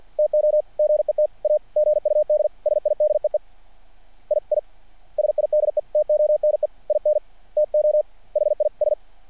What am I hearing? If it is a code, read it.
Morse code, heard as a keyed tone, about 49 wpm, 600 Hz. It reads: TO GET A GUD SIDEE II SIDETONE INTO HIS
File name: archived / CRSnet